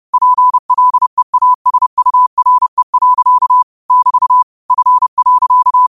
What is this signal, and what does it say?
Morse code, heard as a keyed tone, about 31 wpm, 1000 Hz. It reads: PLEASURE. <BT> F.
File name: qotdfiles / 30wpm